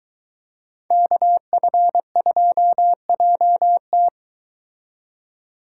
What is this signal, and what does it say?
Morse code, heard as a keyed tone, about 22 wpm, 700 Hz. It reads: KF2JT